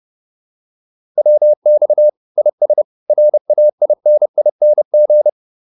Morse code, heard as a keyed tone, about 30 wpm, 600 Hz. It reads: WX IS RAINING
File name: QSO402 30